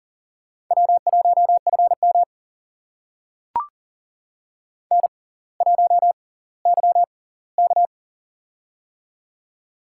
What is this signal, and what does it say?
Morse code, heard as a keyed tone, about 39 wpm, spaced as for 20 wpm, 690 Hz. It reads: W1FM N1YK
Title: USA Call Signs - 4 Characters - 6x Character Spacing - 40wpm